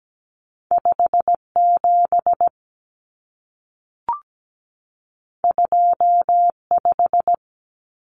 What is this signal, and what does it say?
Morse code, heard as a keyed tone, about 17 wpm, 700 Hz. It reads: 57 25
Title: Common Cut Numbers - 2-Digit 17wpm